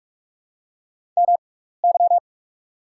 Morse code, heard as a keyed tone, about 44 wpm, spaced as for 15 wpm, 690 Hz. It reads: MY
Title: 6x Character Spacing - Top 100 Words - 45wpm